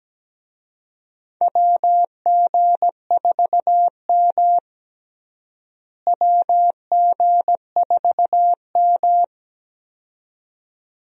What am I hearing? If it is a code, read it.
Morse code, heard as a keyed tone, about 17 wpm, 700 Hz. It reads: WG4M WG4M
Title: USA Call Signs - 4 Characters - Repeated 2x - 17wpm